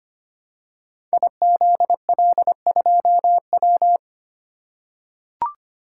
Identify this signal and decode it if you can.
Morse code, heard as a keyed tone, about 25 wpm, 700 Hz. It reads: IZL2W
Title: Code Groups - 5 Letter:Numbers - 25wpm